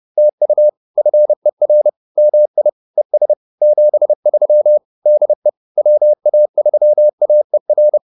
Morse code, heard as a keyed tone, about 30 wpm, 600 Hz. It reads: TU FER MI ES 73 DE WA3AER